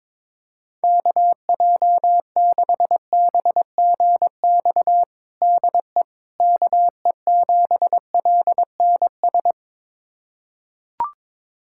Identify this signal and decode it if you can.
Morse code, heard as a keyed tone, about 21 wpm, 700 Hz. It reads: KJ6BGX DE KE7LNS
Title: USA Callsign DE Callsign - 6 Characters - 22wpm